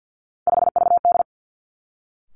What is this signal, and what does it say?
Morse code, heard as a keyed tone, about 50 wpm, 700 Hz. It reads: 5VD
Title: Three Challenging Letter-Number Combinations 50wpm